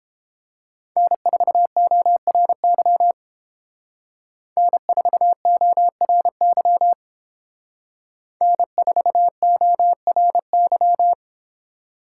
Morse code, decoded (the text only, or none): N4ORY N4ORY N4ORY